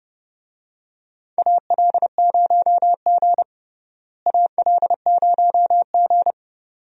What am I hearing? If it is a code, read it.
Morse code, heard as a keyed tone, about 31 wpm, 700 Hz. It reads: AL0G AL0G
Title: USA Call Signs - 4 Characters - Repeated 2x - 30wpm